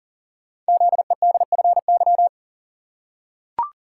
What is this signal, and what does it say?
Morse code, heard as a keyed tone, about 39 wpm, 690 Hz. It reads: ZEDFY
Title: Code Groups - 5 Letters - 40wpm